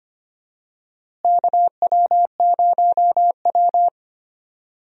Morse code, heard as a keyed tone, about 25 wpm, 700 Hz. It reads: KW0W
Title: USA Call Signs - 4 Characters - Repeated 3x - 25wpm